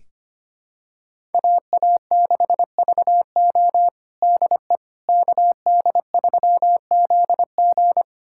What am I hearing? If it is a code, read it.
Morse code, heard as a keyed tone, about 25 wpm, 700 Hz. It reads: AA6VO DE KD3ZG
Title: USA Callsign DE Callsign - 5 Characters - 25wpm